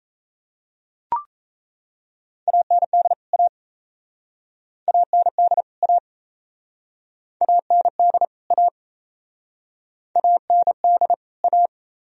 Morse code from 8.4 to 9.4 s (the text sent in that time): A